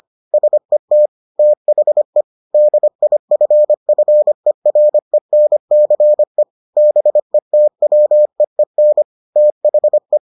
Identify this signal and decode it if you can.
Morse code, heard as a keyed tone, about 25 wpm, 600 Hz. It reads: SET THE DIFFERENCE BETWEEN THE